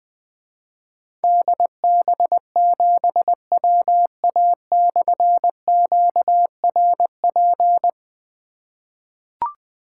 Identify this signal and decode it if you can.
Morse code, heard as a keyed tone, about 21 wpm, 700 Hz. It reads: DB7WA/QRP
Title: Difficult Call Signs 20wpm